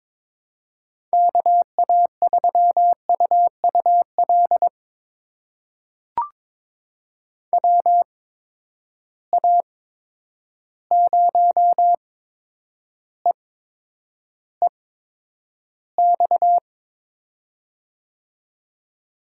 Morse code, heard as a keyed tone, about 22 wpm, spaced as for 9 wpm, 700 Hz. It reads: KA3UUL WA0EEX
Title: USA Call Signs - 6 Characters - 8x Character Spacing - 22wpm